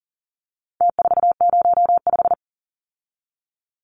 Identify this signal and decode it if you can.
Morse code, heard as a keyed tone, about 40 wpm, 700 Hz. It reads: T405
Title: Numbers - Contest Style - 4-Digit 40wpm